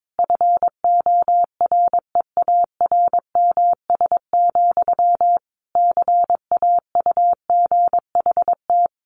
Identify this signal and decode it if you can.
Morse code, heard as a keyed tone, about 22 wpm, 700 Hz. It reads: FOREARMS, CAUGHT
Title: A Sound Of Thunder 22wpm